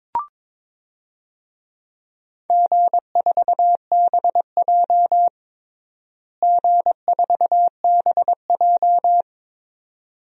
Morse code, heard as a keyed tone, about 22 wpm, 700 Hz. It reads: G4BJ G4BJ
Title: DXCC-100 Call Signs - 4 Characters - Repeated 2x - 22wpm